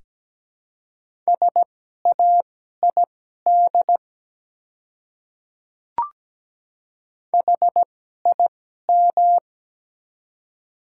Morse code, heard as a keyed tone, about 17 wpm, spaced as for 12 wpm, 700 Hz. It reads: SAID HIM
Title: 2x Character Spacing - Top 100 Words - 17wpm